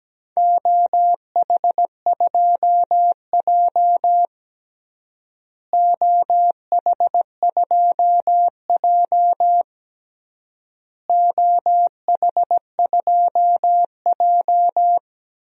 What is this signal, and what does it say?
Morse code, heard as a keyed tone, about 17 wpm, 700 Hz. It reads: OH2J OH2J OH2J
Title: DXCC-100 Call Signs - 4 Characters - Repeated 3x - 17wpm